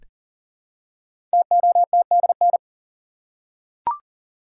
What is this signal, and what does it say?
Morse code, heard as a keyed tone, about 38 wpm, 700 Hz. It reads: TOTDN
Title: Code Groups - 5 Letters - 40wpm